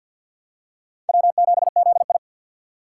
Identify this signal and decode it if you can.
Morse code, heard as a keyed tone, about 47 wpm, 700 Hz. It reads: W7ZI
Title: USA Call Signs - 4 Characters - 3x Character Spacing - 50wpm